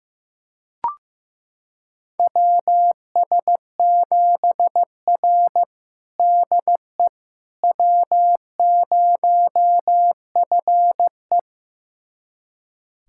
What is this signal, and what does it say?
Morse code, heard as a keyed tone, about 15 wpm, 700 Hz. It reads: WS7R DE W0FE
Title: USA Callsign DE Callsign - 4 Characters - 15wpm